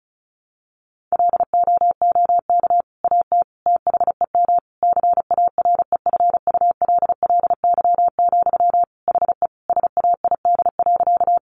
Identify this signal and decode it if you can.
Morse code, heard as a keyed tone, about 34 wpm, 700 Hz. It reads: LOOK AT THEM CAREFULLY, HE SAID.
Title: Sentences from Top 1300 Words 35wpm